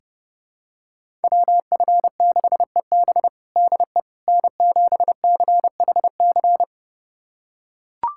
Morse code, heard as a keyed tone, about 30 wpm, 700 Hz. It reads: WF6EB DE N7CHC
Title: USA Callsign DE Callsign - 5 Characters - 30wpm